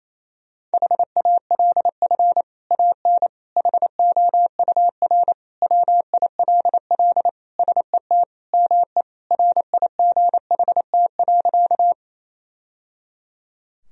Morse code, heard as a keyed tone, about 28 wpm, 700 Hz. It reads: HALF AN HOUR WILL SET ME RIGHT.
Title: Sentences from Top 1000 Words - Review Part 1 - 28wpm